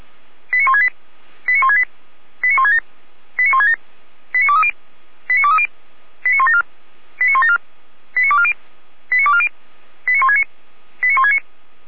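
Five-tone selective call (ZVEI), heard as quick runs of five tones, tones of about 70 ms. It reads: 89168 89168 89167 89167 89230 89230 89174 89174 89240 89240 89169 89169
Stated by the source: ZVEI five-tone selective call received with an SDR.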